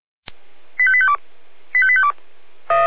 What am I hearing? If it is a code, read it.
Five-tone selective call (ZVEI), heard as quick runs of five tones, tones of about 70 ms. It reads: 85852 85852
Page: /496/SDR/ZVEI/85232/2021.10.21/